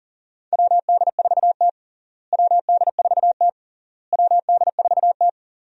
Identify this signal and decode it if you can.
Morse code, heard as a keyed tone, about 40 wpm, 700 Hz. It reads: WD4T WD4T WD4T
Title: USA Call Signs - 4 Characters - Repeated 3x - 40wpm